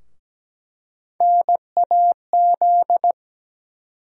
Morse code, heard as a keyed tone, about 17 wpm, 700 Hz. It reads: NAZ
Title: USA Call Sign Postfixes - 3 Letters 17wpm